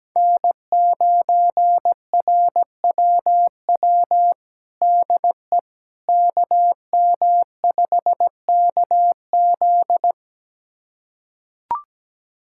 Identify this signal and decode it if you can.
Morse code, heard as a keyed tone, about 17 wpm, 700 Hz. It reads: N9RWW DE KM5KZ
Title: USA Callsign DE Callsign - 5 Characters - 17wpm